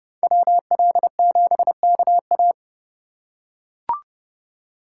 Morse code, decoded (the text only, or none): WL7KA